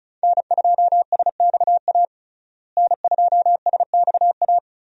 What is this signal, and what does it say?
Morse code, heard as a keyed tone, about 34 wpm, 700 Hz. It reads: N2SXA N2SXA
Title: USA Call Signs - 5 Characters - Repeated 2x - 35wpm